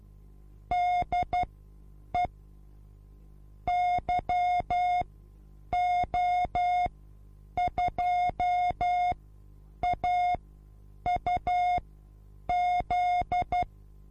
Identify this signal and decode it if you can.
Morse code, heard as a keyed tone, about 12 wpm, spaced as for 8 wpm, 720 Hz. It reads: DE YO2AUZ
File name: repos / Morsekurs / DJ4UF Morsen24